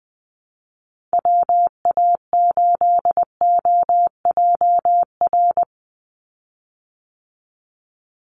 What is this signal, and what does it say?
Morse code, heard as a keyed tone, about 20 wpm, 700 Hz. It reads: WA8OJR
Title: Call Signs Repeated 2x - 20wpm